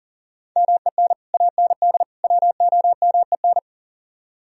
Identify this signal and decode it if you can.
Morse code, heard as a keyed tone, about 48 wpm, 700 Hz. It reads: MEN AND WOMEN
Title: Speed-Racing - Sets of 3 Words 40wpm